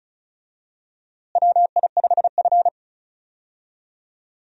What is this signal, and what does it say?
Morse code, heard as a keyed tone, about 36 wpm, 700 Hz. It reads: WI5F
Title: USA Call Signs - 4 Characters - Repeated 3x - 35wpm